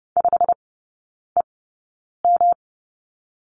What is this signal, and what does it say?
Morse code, heard as a keyed tone, about 30 wpm, spaced as for 9 wpm, 700 Hz. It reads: 5EM
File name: Lesson 18e - Letter m - Three Characters 30wpm